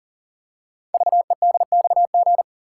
Morse code, heard as a keyed tone, about 38 wpm, 690 Hz. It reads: VEDXG
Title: Code Groups - 5 Letters - 40wpm